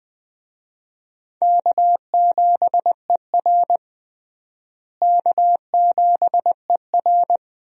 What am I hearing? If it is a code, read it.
Morse code, heard as a keyed tone, about 20 wpm, 700 Hz. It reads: K7ER K7ER
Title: USA Call Signs - 4 Characters - Repeated 2x - 20wpm